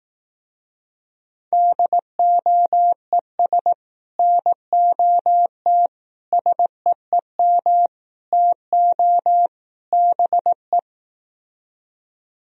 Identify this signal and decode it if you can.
Morse code, heard as a keyed tone, about 18 wpm, 700 Hz. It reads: DOES NOT SEEM TO BE
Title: Speed-Racing - Sets of 5 Words 15wpm